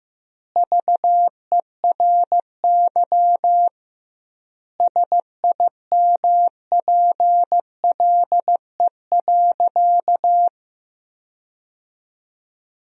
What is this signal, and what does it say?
Morse code, heard as a keyed tone, about 15 wpm, 700 Hz. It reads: VERY SIMPLE.
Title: 2x Word-Spacing - Sentences from Top 400 Words - 15wpm